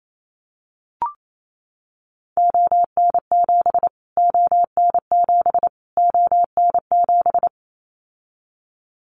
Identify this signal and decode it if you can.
Morse code, heard as a keyed tone, about 28 wpm, 700 Hz. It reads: ON7 ON7 ON7
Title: Callsign Prefixes for Top 25 DXCC - Repeated 3x - 28wpm